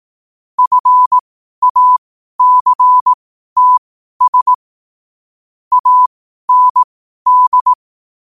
Morse code, heard as a keyed tone, about 18 wpm, spaced as for 12 wpm, 1000 Hz. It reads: FACTS AND
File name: qotdfiles / 12wpm